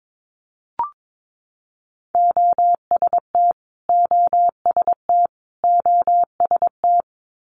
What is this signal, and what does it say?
Morse code, heard as a keyed tone, about 22 wpm, 700 Hz. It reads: OST OST OST
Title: Multi-letter Phonemes Repeated 3x 22wpm